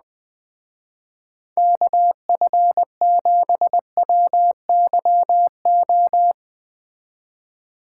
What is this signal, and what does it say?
Morse code, heard as a keyed tone, about 21 wpm, 700 Hz. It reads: KF7WYO